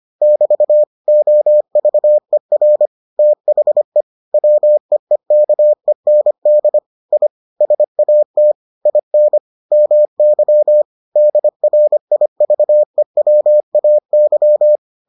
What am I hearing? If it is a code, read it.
Morse code, heard as a keyed tone, about 25 wpm, 600 Hz. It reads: <BT> OVER THE WEEKEND I SAT IN MY DRIVEWAY